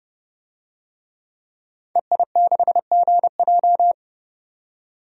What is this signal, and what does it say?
Morse code, heard as a keyed tone, about 29 wpm, 700 Hz. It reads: EI6GJ
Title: DXCC-100 Call Signs - 5 Characters - 30wpm